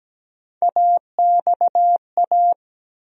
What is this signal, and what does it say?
Morse code, heard as a keyed tone, about 17 wpm, 700 Hz. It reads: AXA
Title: USA Call Sign Postfixes - 3 Letters 17wpm